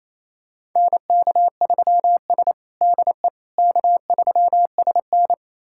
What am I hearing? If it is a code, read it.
Morse code, heard as a keyed tone, about 28 wpm, 700 Hz. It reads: NK3S DE K3SN
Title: USA Callsign DE Callsign - 4 Characters - 28wpm